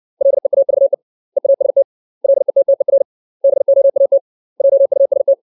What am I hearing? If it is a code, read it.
Morse code, heard as a keyed tone, about 60 wpm, 550 Hz. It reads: LEAVE EAST LETTER BOAT JUST
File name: 632 words / 632 Words 60WPM